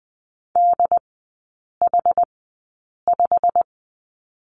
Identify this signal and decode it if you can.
Morse code, heard as a keyed tone, about 20 wpm, spaced as for 8 wpm, 700 Hz. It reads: DH5
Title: 087 Lesson 15e - Number 5 - Three Characters 20wpm